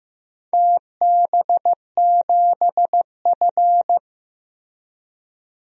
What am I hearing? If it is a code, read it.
Morse code, heard as a keyed tone, about 15 wpm, 700 Hz. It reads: TB7F
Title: DXCC-100 Call Signs - 4 Characters - Repeated 2x - 15wpm